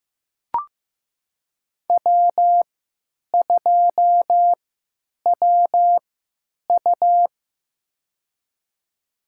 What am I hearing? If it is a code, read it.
Morse code, heard as a keyed tone, about 15 wpm, spaced as for 9 wpm, 700 Hz. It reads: W2WU
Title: USA Call Signs - 4 Characters - 3x Character Spacing - 15wpm